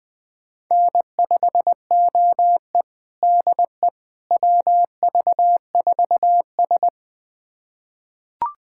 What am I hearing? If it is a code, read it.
Morse code, heard as a keyed tone, about 20 wpm, 700 Hz. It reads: N5OE DE WV4S
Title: USA Callsign DE Callsign - 4 Characters - 20wpm